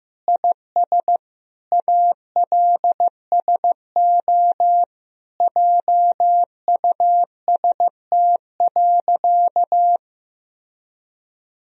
Morse code, heard as a keyed tone, about 15 wpm, 700 Hz. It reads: IS ALSO JUST.
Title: Sentences from Top 1000 Words - Review Part 1 - 15wpm